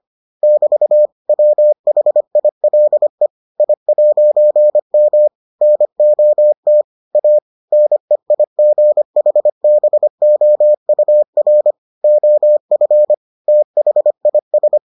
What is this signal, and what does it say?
Morse code, heard as a keyed tone, about 25 wpm, 600 Hz. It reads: <BT> WHILE I'M NOT A NEIGHBOUR OF THIS